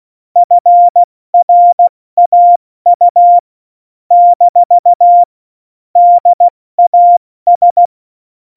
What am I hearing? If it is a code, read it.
Morse code, heard as a keyed tone, about 16 wpm, spaced as for 14 wpm, 700 Hz. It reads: FRAU - DAS
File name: Morsekurs / Witze / Witz 23. April 2019, 70 F80